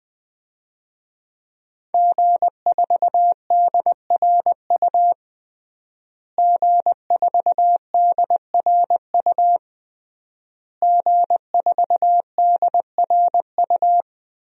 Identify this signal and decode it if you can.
Morse code, heard as a keyed tone, about 20 wpm, 700 Hz. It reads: G4DRU G4DRU G4DRU
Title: DXCC-100 Call Signs - 5 Characters - Repeated 3x - 20wpm